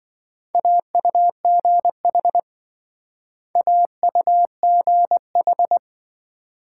Morse code, heard as a keyed tone, about 25 wpm, 700 Hz. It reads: AUGH AUGH
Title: Speed-Racing Multi-letter Phonemes - 20wpm